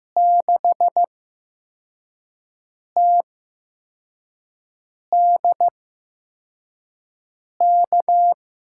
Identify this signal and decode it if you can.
Morse code, heard as a keyed tone, about 15 wpm, spaced as for 4 wpm, 700 Hz. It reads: 6TDK
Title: USA Call Signs - 8x Character Spacing - 15wpm